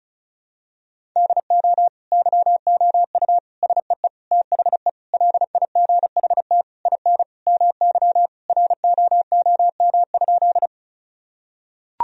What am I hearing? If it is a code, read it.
Morse code, heard as a keyed tone, about 34 wpm, 700 Hz. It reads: DO YOU SEE THE LIGHT IN MY ROOM?
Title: Sentences from Top 1000 Words - Review Part 1 - 35wpm